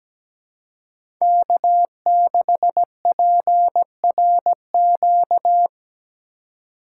Morse code, heard as a keyed tone, about 17 wpm, 700 Hz. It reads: K6PRQ
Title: USA Call Signs - 5 Characters - 2x Character Spacing - 17wpm